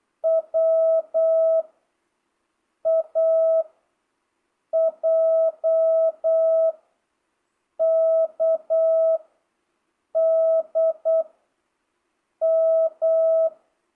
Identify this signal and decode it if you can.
Morse code, heard as a keyed tone, about 8 wpm, 650 Hz. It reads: W A J K D M
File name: jh8cbh / 2025cw